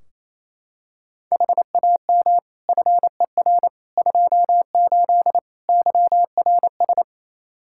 Morse code, heard as a keyed tone, about 28 wpm, 700 Hz. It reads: HAM FER 28 YRS
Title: K1USN SST QSO Practice 28wpm